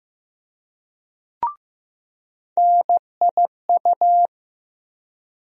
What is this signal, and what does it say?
Morse code, heard as a keyed tone, about 15 wpm, 700 Hz. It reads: NIU
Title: USA Call Sign Postfixes - 3 Letters 15wpm